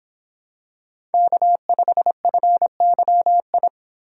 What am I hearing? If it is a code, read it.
Morse code, heard as a keyed tone, about 26 wpm, 700 Hz. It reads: K5FYI